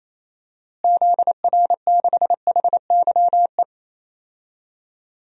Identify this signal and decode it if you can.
Morse code, heard as a keyed tone, about 27 wpm, 700 Hz. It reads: ZR6HYE
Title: DXCC-100 Call Signs - 6 Characters - Repeated 2x - 28wpm